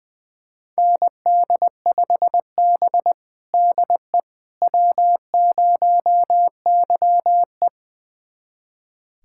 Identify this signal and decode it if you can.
Morse code, heard as a keyed tone, about 20 wpm, 700 Hz. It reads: ND5B DE W0YE
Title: USA Callsign DE Callsign - 4 Characters - 20wpm